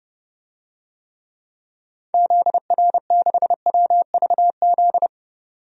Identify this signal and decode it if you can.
Morse code, heard as a keyed tone, about 30 wpm, 700 Hz. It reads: ZR6WVZ
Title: DXCC-100 Call Signs - 6 Characters - 30wpm